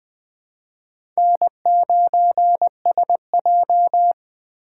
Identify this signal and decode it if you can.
Morse code, heard as a keyed tone, about 21 wpm, 700 Hz. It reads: N9SJ
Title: USA Call Signs - 4 Characters - 3x Character Spacing - 20wpm